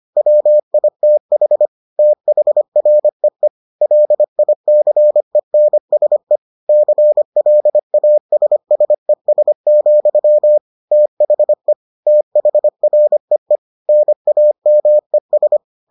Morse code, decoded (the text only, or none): WITH THREE LICENSE CLASSES, THE THREE NAMES